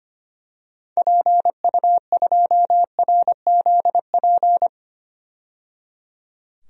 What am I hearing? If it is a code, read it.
Morse code, heard as a keyed tone, about 25 wpm, 700 Hz. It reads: PU2RZP
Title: DXCC-100 Call Signs - 6 Characters - 25wpm